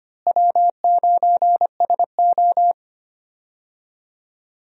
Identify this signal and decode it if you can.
Morse code, heard as a keyed tone, about 25 wpm, 700 Hz. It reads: W9SO
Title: USA Call Signs - 4 Characters - Repeated 3x - 25wpm